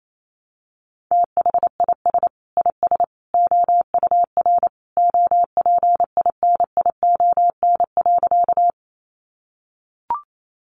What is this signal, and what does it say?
Morse code, heard as a keyed tone, about 28 wpm, 700 Hz. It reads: THIS IS OUR OPINION.
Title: Sentences from Top 1200 Words 28wpm